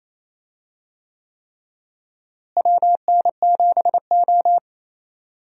Morse code, heard as a keyed tone, about 28 wpm, 700 Hz. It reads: WN7O
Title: USA Call Signs - 4 Characters - Repeated 3x - 28wpm